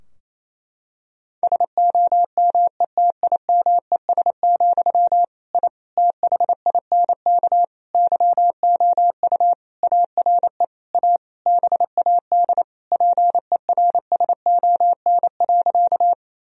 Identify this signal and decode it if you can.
Morse code, heard as a keyed tone, about 28 wpm, 700 Hz. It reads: SOMETIMES, I THINK YOU ARE A BAD PERSON.
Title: Sentences from Top 1000 Words - Review Part 3 - 28wpm